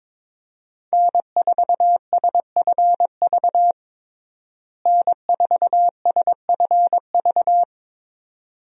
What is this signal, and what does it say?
Morse code, heard as a keyed tone, about 22 wpm, 700 Hz. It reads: N4SFV N4SFV
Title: USA Call Signs - 5 Characters - Repeated 2x - 22wpm